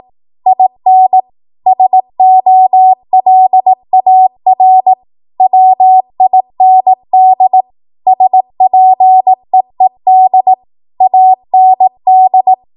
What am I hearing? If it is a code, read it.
Morse code, heard as a keyed tone, about 18 wpm, 750 Hz. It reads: IN SOLAR WIND SPEED AND
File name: Archive / 18 WPM / 18WPM